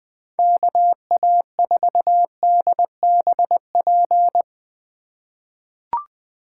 Morse code, heard as a keyed tone, about 20 wpm, 700 Hz. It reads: KA4DBP
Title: USA Call Signs - 6 Characters - 4x Character Spacing - 20wpm